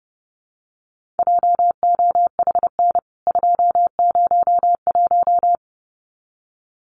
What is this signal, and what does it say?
Morse code, heard as a keyed tone, about 28 wpm, 700 Hz. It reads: JOHN 201
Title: CWT Exchange 30wpm